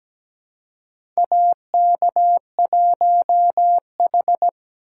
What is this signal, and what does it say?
Morse code, heard as a keyed tone, about 17 wpm, 700 Hz. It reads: AK1H
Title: USA Call Signs - 4x Character Spacing - 17wpm